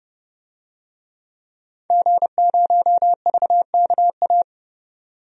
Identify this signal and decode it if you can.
Morse code, heard as a keyed tone, about 30 wpm, 700 Hz. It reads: G0VKA
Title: DXCC-100 Call Signs - 5 Characters - 30wpm